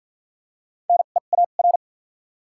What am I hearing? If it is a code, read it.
Morse code, heard as a keyed tone, about 53 wpm, spaced as for 37 wpm, 700 Hz. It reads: NEAR
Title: 2x Character Spacing - Top 500 Words - 50wpm